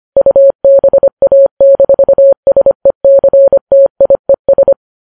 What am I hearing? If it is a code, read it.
Morse code, heard as a keyed tone, about 25 wpm, 550 Hz. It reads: UBA-SECTIES
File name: Files Web tls / Morse on9fv / morse25kps